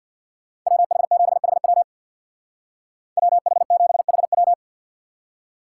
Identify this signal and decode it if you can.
Morse code, heard as a keyed tone, about 62 wpm, 700 Hz. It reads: WH7HW WH7HW
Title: Speed Racing - USA 2x2 Call Signs 50wpm